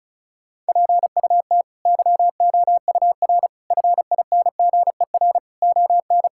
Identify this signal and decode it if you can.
Morse code, heard as a keyed tone, about 36 wpm, 700 Hz. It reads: PUT YOUR FINGER ON
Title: Sentences from Top 1600 Words 35wpm